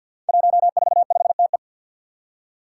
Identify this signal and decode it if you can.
Morse code, heard as a keyed tone, about 53 wpm, 700 Hz. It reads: 145TE
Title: Five Challenging Letter-Number Combinations 50wpm